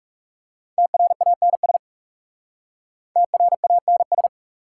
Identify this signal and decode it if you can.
Morse code, heard as a keyed tone, about 38 wpm, 690 Hz. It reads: TRANS TRANS